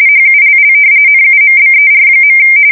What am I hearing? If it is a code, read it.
RTTY, about 45.45 baud. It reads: HULL NUMBER.
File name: fsk170 / battleshipnj